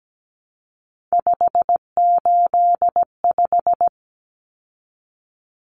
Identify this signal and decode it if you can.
Morse code, heard as a keyed tone, about 17 wpm, 700 Hz. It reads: 585
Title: Numbers - 3-Digit 17wpm